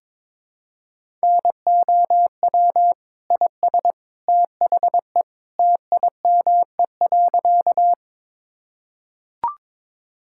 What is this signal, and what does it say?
Morse code, heard as a keyed tone, about 22 wpm, 700 Hz. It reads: NOW IS THE TIME.
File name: practice / 22wpm / Sentences with 2-Word Phrases - 22wpm